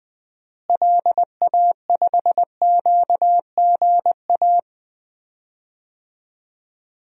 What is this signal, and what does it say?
Morse code, heard as a keyed tone, about 20 wpm, 700 Hz. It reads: LA5QGA
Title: DXCC-100 Call Signs - 6 Characters - Repeated 2x - 20wpm